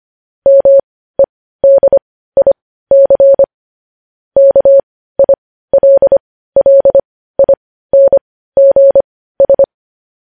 Morse code, heard as a keyed tone, about 25 wpm, spaced as for 15 wpm, 550 Hz. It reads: MEDIC KILLINGS
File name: podcast / News headlines-1x-25x15